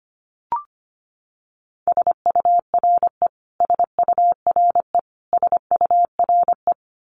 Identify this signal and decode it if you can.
Morse code, heard as a keyed tone, about 25 wpm, 700 Hz. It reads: SURE SURE SURE
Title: Top 500 Words Repeated 25wpm